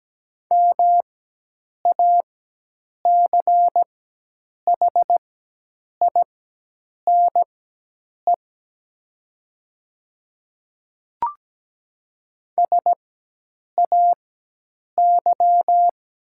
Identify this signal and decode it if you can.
Morse code, heard as a keyed tone, about 17 wpm, spaced as for 8 wpm, 700 Hz. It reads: MACHINE SAY
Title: 4x Character Spacing - Top 500 Words - 17wpm